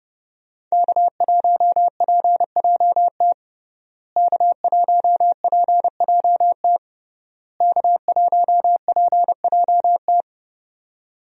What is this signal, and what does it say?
Morse code, heard as a keyed tone, about 30 wpm, 700 Hz. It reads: K1PJT K1PJT K1PJT
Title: USA Call Signs - 5 Characters - Repeated 3x - 30wpm